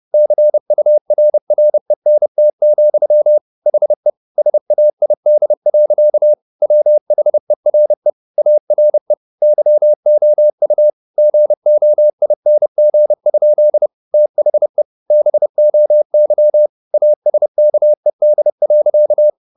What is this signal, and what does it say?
Morse code, heard as a keyed tone, about 30 wpm, 600 Hz. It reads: CURRENT, HE SAID. WHERE ARE YOU GOING? THE BOY ASKED.
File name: the old man and the sea / the old man and the sea 1 30wpm